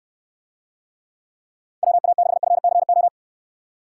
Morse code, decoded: WA6FCO

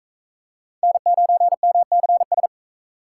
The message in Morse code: N9MCS

